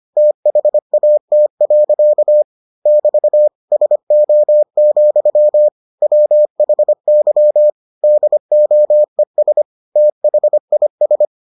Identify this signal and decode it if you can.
Morse code, heard as a keyed tone, about 25 wpm, 600 Hz. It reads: THAT. <BT> SO, WHY DOES THIS